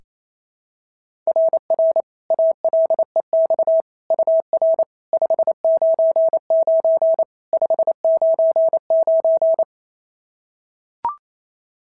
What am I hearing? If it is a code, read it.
Morse code, heard as a keyed tone, about 28 wpm, 690 Hz. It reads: RR ALEX UR 599 599
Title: SOTA QSO Practice 28wpm